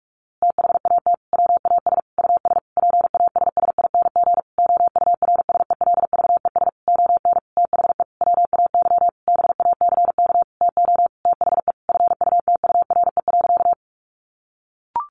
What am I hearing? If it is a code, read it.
Morse code, heard as a keyed tone, about 44 wpm, 700 Hz. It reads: THAT WAS US PASSING OURSELVES ON THE WAY BACK TO THE FUTURE.
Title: A Sound Of Thunder 45wpm